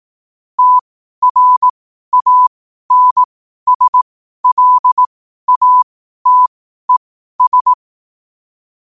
Morse code, heard as a keyed tone, about 18 wpm, spaced as for 12 wpm, 1000 Hz. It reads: TRANSLATES
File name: qotdfiles / 12wpm